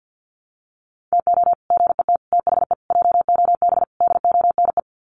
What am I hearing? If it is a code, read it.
Morse code, heard as a keyed tone, about 50 wpm, 700 Hz. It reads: TO GET THE JOB DONE
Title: Sets of 5 Words Encore2 50wpm